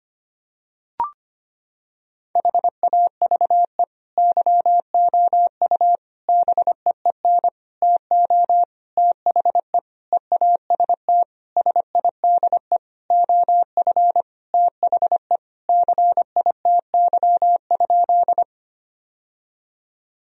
Morse code, heard as a keyed tone, about 25 wpm, 700 Hz. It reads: HAVE YOU BEEN TO THE EAST SIDE OF THE CITY?
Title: Sentences from Top 1000 Words - Review Part 3 - 25wpm